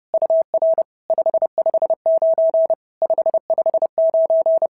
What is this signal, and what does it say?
Morse code, heard as a keyed tone, about 30 wpm, 650 Hz. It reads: UR 559 559
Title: SOTA QSO Practice 30wpm